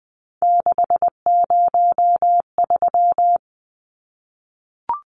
Morse code, heard as a keyed tone, about 20 wpm, 700 Hz. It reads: 603
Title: Numbers - 3-Digit 20wpm